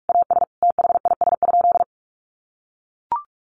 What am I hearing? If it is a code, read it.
Morse code, heard as a keyed tone, about 44 wpm, 700 Hz. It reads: AS THIS?
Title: Sentences from Top 400 Words 45wpm